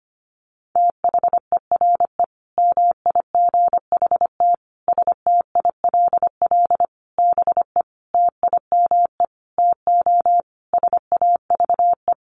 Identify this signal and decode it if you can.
Morse code, heard as a keyed tone, about 25 wpm, 700 Hz. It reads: THERE MIGHT STILL BE TIME TO SAVE
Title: Sentences from Top 1100 Words 25wpm